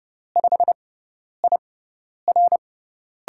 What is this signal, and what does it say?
Morse code, heard as a keyed tone, about 32 wpm, spaced as for 10 wpm, 700 Hz. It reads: 5IR
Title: USA Call Signs - 6x Character Spacing - 30wpm